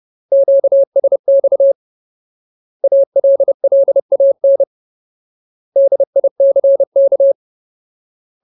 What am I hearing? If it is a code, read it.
Morse code, heard as a keyed tone, about 30 wpm, 550 Hz. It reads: QSX ALLAN DICK
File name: Ultimate Word List / 30WPM Ultimate